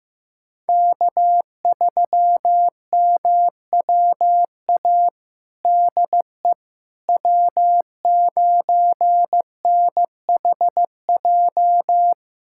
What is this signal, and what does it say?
Morse code, heard as a keyed tone, about 15 wpm, 700 Hz. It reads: K3MWA DE W9NHJ
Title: USA Callsign DE Callsign - 5 Characters - 15wpm